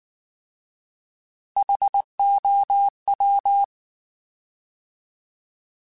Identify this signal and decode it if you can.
Morse code, heard as a keyed tone, about 19 wpm, 800 Hz. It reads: HOW